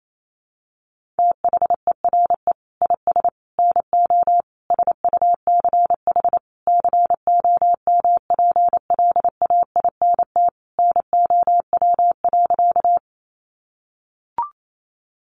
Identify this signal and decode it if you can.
Morse code, heard as a keyed tone, about 28 wpm, 700 Hz. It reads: THERE IS NO SUCH COMPLAINT NOW.
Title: Sentences from Top 1400 Words 28wpm